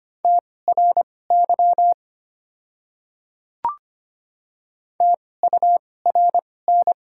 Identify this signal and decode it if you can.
Morse code, heard as a keyed tone, about 25 wpm, spaced as for 18 wpm, 700 Hz. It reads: TRY TURN
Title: 2x Character Spacing - Top 200 Words - 25wpm